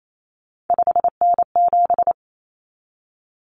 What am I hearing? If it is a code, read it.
Morse code, heard as a keyed tone, about 28 wpm, 700 Hz. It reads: 5N7
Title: Common Cut Numbers - 3-Digit 28wpm